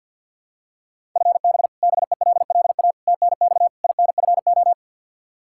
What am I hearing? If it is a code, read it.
Morse code, heard as a keyed tone, about 49 wpm, 700 Hz. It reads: FB BELLA TNX INFO
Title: Practice QSO - Exchange 3 - More Info - 50wpm